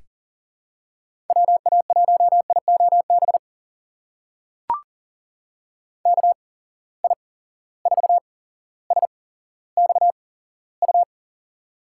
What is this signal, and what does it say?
Morse code, heard as a keyed tone, about 40 wpm, spaced as for 17 wpm, 700 Hz. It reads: WA1IOB KI4SXU
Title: USA Call Signs - 6 Characters - 8x Character Spacing - 40wpm